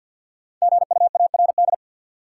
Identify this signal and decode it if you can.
Morse code, heard as a keyed tone, about 46 wpm, 700 Hz. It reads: GUARD
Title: Sentences from Top 700 Words - New Word Drill - 50wpm